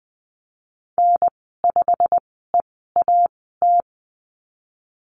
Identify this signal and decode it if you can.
Morse code, heard as a keyed tone, about 20 wpm, spaced as for 15 wpm, 700 Hz. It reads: N5EAT
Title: 088 Lesson 15g - Number 5 - Call Signs 20wpm